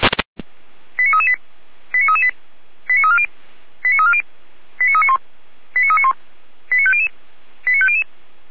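Five-tone selective call (ZVEI), heard as quick runs of five tones, tones of about 70 ms. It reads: 89338 89338 89340 89340 89391 89391 89500 89500
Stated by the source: ZVEI five-tone selective call 89320, received by SDR.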